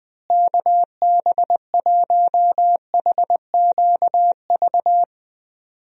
Morse code, heard as a keyed tone, about 19 wpm, 700 Hz. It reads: KB1HQV
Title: USA Call Signs - 6 Characters - Repeated 2x - 20wpm